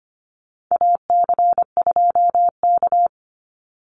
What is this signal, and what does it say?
Morse code, heard as a keyed tone, about 25 wpm, 700 Hz. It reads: AC2K
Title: Call Signs Repeated 2x - 25wpm